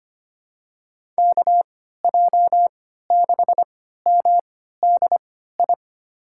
Morse code, heard as a keyed tone, about 25 wpm, spaced as for 14 wpm, 700 Hz. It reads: KJ6MDI
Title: USA Call Signs - 6 Characters - 3x Character Spacing - 25wpm